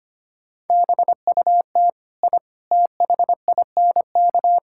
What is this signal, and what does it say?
Morse code, heard as a keyed tone, about 25 wpm, 700 Hz. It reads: BUT I THINK
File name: practice / 25wpm / Sentences with 3-Word Phrases - 25wpm